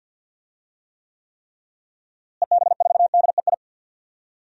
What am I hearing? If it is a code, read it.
Morse code, heard as a keyed tone, about 49 wpm, 700 Hz. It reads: EB4DEI
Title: DXCC-100 Call Signs - 6 Characters - Repeated 3x - 50wpm